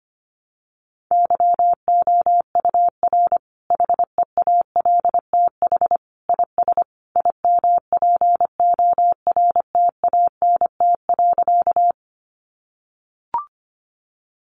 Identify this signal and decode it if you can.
Morse code, heard as a keyed tone, about 25 wpm, 700 Hz. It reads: YOUR HEALTH IS IMPORTANT.
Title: Sentences from Top 600 Words 25wpm